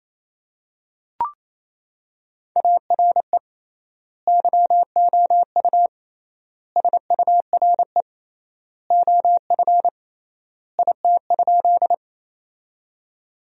Morse code, heard as a keyed tone, about 27 wpm, 700 Hz. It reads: ARE YOU SURE OF IT?
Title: 3x Word-Spacing - Sentences from Top 300 Words - 28wpm